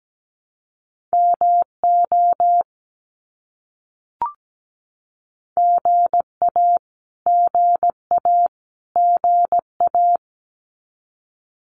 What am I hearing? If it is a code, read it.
Morse code, heard as a keyed tone, about 17 wpm, 700 Hz. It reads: MO GA GA GA
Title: US States Repeated 3x 17wpm